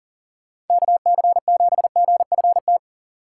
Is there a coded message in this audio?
Yes: Morse code from 0.7 to 2.8 s, heard as a keyed tone, about 38 wpm, 700 Hz. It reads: KC7GFT